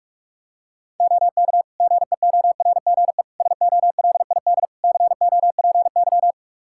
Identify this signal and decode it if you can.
Morse code, heard as a keyed tone, about 46 wpm, 700 Hz. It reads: OK GEORGE SOLID COPY
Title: K1USN SST QSO Practice 45wpm